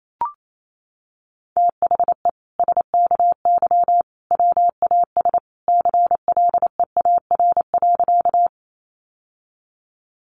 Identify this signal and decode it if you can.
Morse code, heard as a keyed tone, about 28 wpm, 700 Hz. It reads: THE SKY WAS CLEAR.
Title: Sentences from Top 1200 Words 28wpm